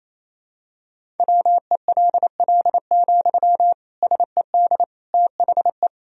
Morse code, heard as a keyed tone, about 27 wpm, 700 Hz. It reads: WELL, SED THE
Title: Sentences from Top 400 Words and Abbreviations 28wpm